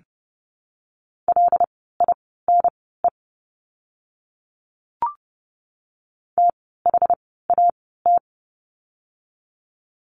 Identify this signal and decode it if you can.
Morse code, heard as a keyed tone, about 30 wpm, spaced as for 17 wpm, 700 Hz. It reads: LINE THAT
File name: course / 30:10wpm / 115 Lesson 19f - Letter w - Words 30wpm